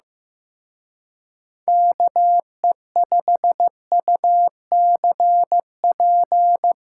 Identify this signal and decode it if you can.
Morse code, heard as a keyed tone, about 16 wpm, 700 Hz. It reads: KE5UCP
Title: Speed Racing - USA Call Signs 15wpm